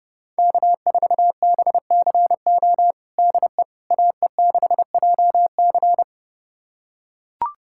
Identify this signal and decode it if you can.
Morse code, heard as a keyed tone, about 30 wpm, 700 Hz. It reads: K4BCO DE AE6JC
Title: USA Callsign DE Callsign - 5 Characters - 30wpm